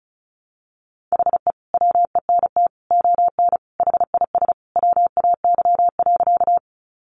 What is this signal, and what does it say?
Morse code, heard as a keyed tone, about 34 wpm, 700 Hz. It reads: HE WENT ON HIS WAY.
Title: Sentences from Top 200 Words 35wpm